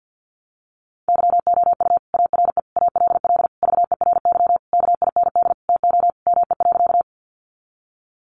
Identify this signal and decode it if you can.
Morse code, heard as a keyed tone, about 50 wpm, 700 Hz. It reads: YOU ARE ALL VERY KIND TO ME.
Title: Sentences from Top 200 Words 50wpm